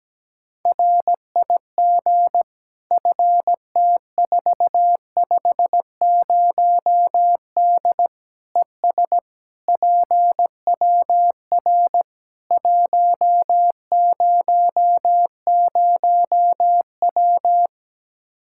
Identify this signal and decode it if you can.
Morse code, heard as a keyed tone, about 17 wpm, 700 Hz. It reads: RIG FT450D ES PWR 100W
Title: Practice QSO - Exchange 2 - Info - 17wpm